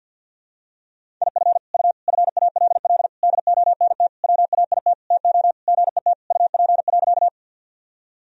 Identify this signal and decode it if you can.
Morse code, heard as a keyed tone, about 49 wpm, 700 Hz. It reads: IF U FALL DONT WAIT TO GET UP.
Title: Sentences from Top 500 Words and Abbreviations 50wpm